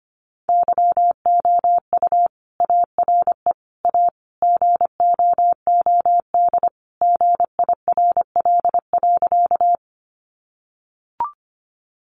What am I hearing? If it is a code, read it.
Morse code, heard as a keyed tone, about 25 wpm, 700 Hz. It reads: YOU ARE A GOOD GIRL.
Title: Sentences from Top 400 Words 25wpm